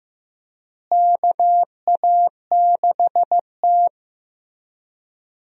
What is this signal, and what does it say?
Morse code, heard as a keyed tone, about 15 wpm, 700 Hz. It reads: KA6T
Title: USA Call Signs - 4 Characters - Repeated 2x - 15wpm